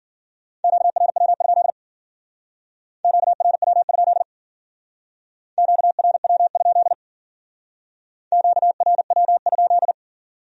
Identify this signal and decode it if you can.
Morse code, heard as a keyed tone, about 51 wpm, 700 Hz. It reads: QRW? QRW? QRW? QRW?